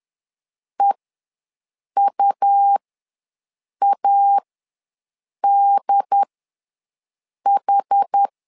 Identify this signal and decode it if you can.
Morse code, heard as a keyed tone, about 10 wpm, spaced as for 6 wpm, 800 Hz. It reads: EUADH
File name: LEZIONI / 2 UHDN / 2 UHD ws 10wpm 2058